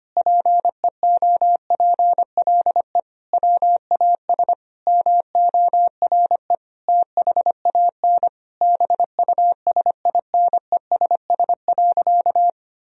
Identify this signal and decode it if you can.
Morse code, heard as a keyed tone, about 25 wpm, 700 Hz. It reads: PEOPLE WAS MORE THAN BUSINESS.
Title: Sentences from Top 1000 Words - Review Part 2 - 25wpm